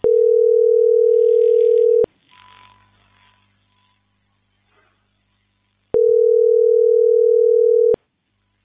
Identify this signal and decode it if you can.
Ringback tone (North America), 440 + 480 Hz.